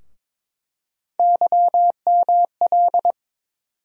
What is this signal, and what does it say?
Morse code, heard as a keyed tone, about 22 wpm, 700 Hz. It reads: YML
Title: USA Call Sign Postfixes - 3 Letters 22wpm